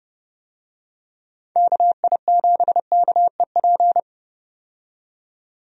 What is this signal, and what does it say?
Morse code, heard as a keyed tone, about 30 wpm, 700 Hz. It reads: KI7KEP
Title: USA Call Signs - 6 Characters - 30wpm